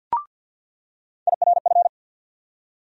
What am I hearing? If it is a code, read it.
Morse code, heard as a keyed tone, about 47 wpm, 700 Hz. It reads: IRF
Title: USA Call Sign Postfixes - 3 Letters 50wpm